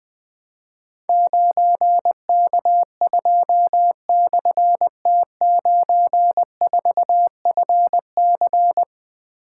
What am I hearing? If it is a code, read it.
Morse code, heard as a keyed tone, about 20 wpm, 700 Hz. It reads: 9K2/T94FC
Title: Difficult Call Signs 20wpm